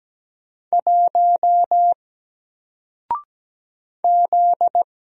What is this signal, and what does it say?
Morse code, heard as a keyed tone, about 17 wpm, 700 Hz. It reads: 1 Z
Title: Single Character (Letter:Number:Punctuation:Prosign) Alt 17wpm